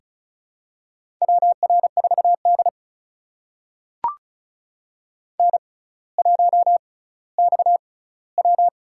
Morse code, heard as a keyed tone, about 36 wpm, spaced as for 18 wpm, 700 Hz. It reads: WR4D N1XW
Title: USA Call Signs - 4 Characters - 6x Character Spacing - 35wpm